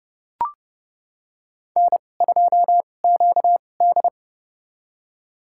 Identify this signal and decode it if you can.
Morse code, heard as a keyed tone, about 30 wpm, spaced as for 22 wpm, 700 Hz. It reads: N2QD
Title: USA Call Signs - 4 Characters - 2x Character Spacing - 30wpm